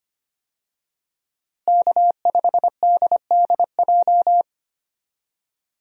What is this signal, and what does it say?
Morse code, heard as a keyed tone, about 25 wpm, 700 Hz. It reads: K5DDJ
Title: USA Call Signs - 5 Characters - Repeated 2x - 25wpm